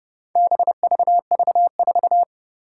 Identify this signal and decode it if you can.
Morse code, heard as a keyed tone, about 30 wpm, 700 Hz. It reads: BVV4
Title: Four Challenging Letter-Number Combinations 30wpm